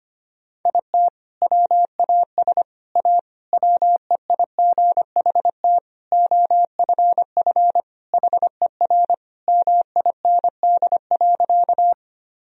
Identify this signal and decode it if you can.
Morse code, heard as a keyed tone, about 25 wpm, 700 Hz. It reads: IT WAS A WEIGHT OFF HER MIND.
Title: Sentences from Top 1000 Words - Review Part 3 - 25wpm